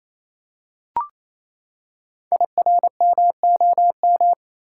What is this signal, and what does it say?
Morse code, heard as a keyed tone, about 28 wpm, 700 Hz. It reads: IRMOM
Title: Code Groups - 5 Letters - 28wpm